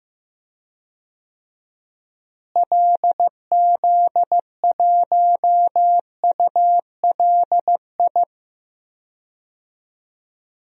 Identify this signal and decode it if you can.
Morse code, heard as a keyed tone, about 15 wpm, 700 Hz. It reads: LZ1ULI
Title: DXCC-100 Call Signs - 6 Characters - Repeated 2x - 15wpm